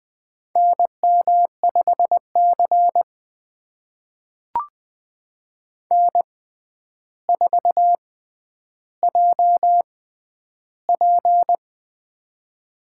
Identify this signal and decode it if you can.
Morse code, heard as a keyed tone, about 20 wpm, spaced as for 10 wpm, 700 Hz. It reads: NM5C N4JP
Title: USA Call Signs - 4 Characters - 6x Character Spacing - 20wpm